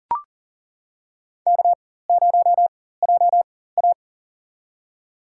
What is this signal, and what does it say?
Morse code, heard as a keyed tone, about 39 wpm, spaced as for 18 wpm, 700 Hz. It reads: K0JA
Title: USA Call Signs - 4x Character Spacing - 40wpm